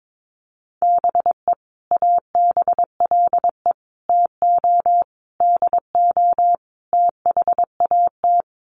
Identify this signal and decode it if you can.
Morse code, heard as a keyed tone, about 22 wpm, 700 Hz. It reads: BE ABLE TO DO THAT
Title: Sets of 5 Words Encore2 22wpm